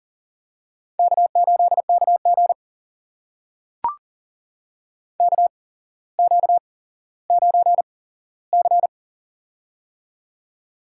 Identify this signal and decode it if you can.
Morse code, heard as a keyed tone, about 40 wpm, spaced as for 17 wpm, 700 Hz. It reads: K8KG KQ9C
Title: USA Call Signs - 4 Characters - 8x Character Spacing - 40wpm